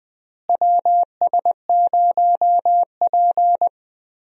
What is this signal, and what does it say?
Morse code, heard as a keyed tone, about 20 wpm, 700 Hz. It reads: WS0P
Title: USA Call Signs - 4 Characters - Repeated 3x - 20wpm